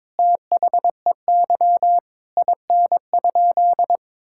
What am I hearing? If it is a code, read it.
Morse code, heard as a keyed tone, about 21 wpm, 700 Hz. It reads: THEY IN?
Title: Sentences from Top 1000 Words - Review Part 5 - 22wpm